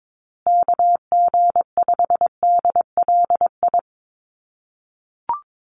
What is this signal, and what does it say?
Morse code, heard as a keyed tone, about 22 wpm, 700 Hz. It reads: KG5DLI
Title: USA Call Signs Repeated 3x 22wpm